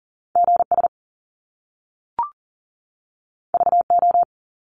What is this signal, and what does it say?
Morse code, heard as a keyed tone, about 40 wpm, 700 Hz. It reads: GS VO
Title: Top Two Letter Combinations in Words 40wpm